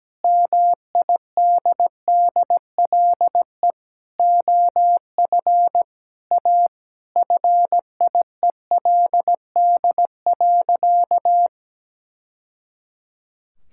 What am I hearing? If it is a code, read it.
Morse code, heard as a keyed tone, about 17 wpm, 700 Hz. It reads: MIDDLE OF A FIELD.
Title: Sentences from Top 1000 Words - Review Part 6 - 17wpm